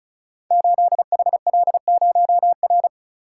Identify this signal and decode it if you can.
Morse code, heard as a keyed tone, about 36 wpm, 690 Hz. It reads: 8HL0R
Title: Code Groups - 5 Letter:Numbers - 35wpm